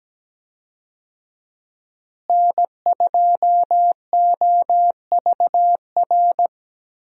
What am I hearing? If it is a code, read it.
Morse code, heard as a keyed tone, about 17 wpm, 700 Hz. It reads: N2OVR
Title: USA Call Signs - 5 Characters - Repeated 3x - 17wpm